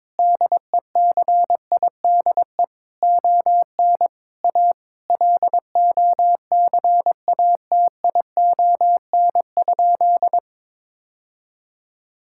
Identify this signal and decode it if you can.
Morse code, heard as a keyed tone, about 21 wpm, 700 Hz. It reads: DECIDE ON A LOCATION?
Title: Sentences from Top 1000 Words - Review Part 6 - 22wpm